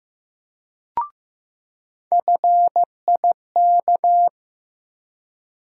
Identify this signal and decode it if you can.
Morse code, heard as a keyed tone, about 15 wpm, 700 Hz. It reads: FIK
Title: USA Call Sign Postfixes - 3 Letters 15wpm